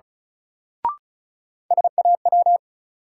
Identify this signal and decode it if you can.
Morse code, heard as a keyed tone, about 36 wpm, 700 Hz. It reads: SAW
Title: Sentences from Top 300 Words - New Word Drill - 35wpm